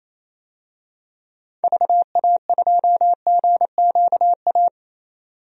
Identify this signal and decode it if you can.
Morse code, heard as a keyed tone, about 28 wpm, 700 Hz. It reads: VA2GQA
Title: DXCC-100 Call Signs - 6 Characters - 28wpm